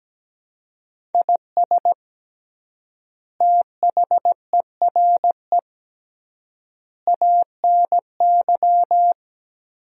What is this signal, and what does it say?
Morse code, heard as a keyed tone, about 17 wpm, 700 Hz. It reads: IS THERE ANY
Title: 3x Word-Spacing - Sentences from Top 300 Words - 17wpm